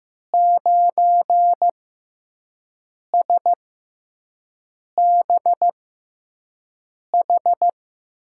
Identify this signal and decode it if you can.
Morse code, heard as a keyed tone, about 15 wpm, spaced as for 5 wpm, 700 Hz. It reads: 9SBH
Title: USA Call Signs - 6 Characters - 6x Character Spacing - 15wpm